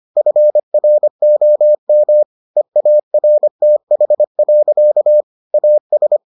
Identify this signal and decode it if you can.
Morse code, heard as a keyed tone, about 25 wpm, 600 Hz. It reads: FROM EARTH. AS